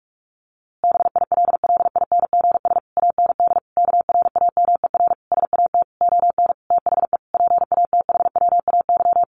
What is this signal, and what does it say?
Morse code, heard as a keyed tone, about 44 wpm, 700 Hz. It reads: BILLINGS AND KRAMER SAT ON THE PATHWAY
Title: A Sound Of Thunder 45wpm